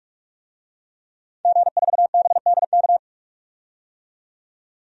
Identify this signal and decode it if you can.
Morse code, heard as a keyed tone, about 48 wpm, 700 Hz. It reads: G4BDK
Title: DXCC-100 Call Signs - 5 Characters - 45wpm